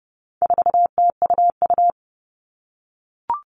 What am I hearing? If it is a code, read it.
Morse code, heard as a keyed tone, about 30 wpm, 700 Hz. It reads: 4TUU
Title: All Cut Numbers - 4-Digit 30wpm